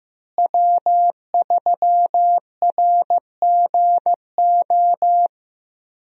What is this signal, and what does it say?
Morse code, heard as a keyed tone, about 15 wpm, 700 Hz. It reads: W3RGO
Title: USA Call Signs - 5 Characters - Repeated 3x - 15wpm